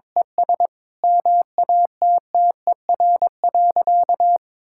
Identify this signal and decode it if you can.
Morse code, heard as a keyed tone, about 22 wpm, 700 Hz. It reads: ES MATTER.
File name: practice / 22wpm / Sentences with 4-Word Phrases - 22wpm